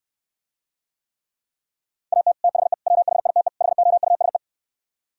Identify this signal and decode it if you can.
Morse code, heard as a keyed tone, about 67 wpm, 700 Hz. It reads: AT THE WHITE HOUSE